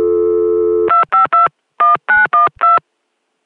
DTMF, heard as telephone touch tones, tones of about 150 ms. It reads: 2521912